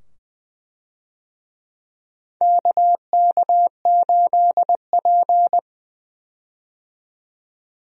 Morse code, heard as a keyed tone, about 20 wpm, 700 Hz. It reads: KK8P
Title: USA Call Signs - 4 Characters - 20wpm